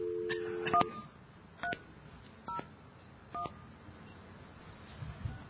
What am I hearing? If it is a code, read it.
DTMF, heard as telephone touch tones, tones of about 120 ms. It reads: 1301